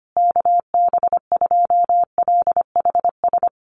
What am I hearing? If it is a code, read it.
Morse code, heard as a keyed tone, about 25 wpm, 700 Hz. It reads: KB2LHS
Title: USA Call Signs Repeated 3x 25wpm